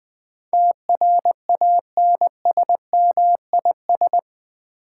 Morse code, heard as a keyed tone, about 20 wpm, 700 Hz. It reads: TRANSMIS
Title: Build A Word - 4 Letters At A Time - 12-Letter Words - 20wpm